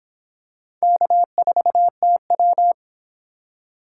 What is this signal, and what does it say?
Morse code, heard as a keyed tone, about 24 wpm, 700 Hz. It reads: K4TW